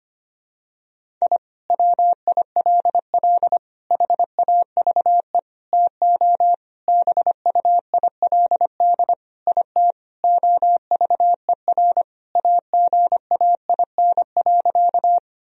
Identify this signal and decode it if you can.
Morse code, heard as a keyed tone, about 25 wpm, 700 Hz. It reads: I WILL HAVE TO BUILD IT OVER AGAIN.
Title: Sentences from Top 1000 Words - Review Part 1 - 25wpm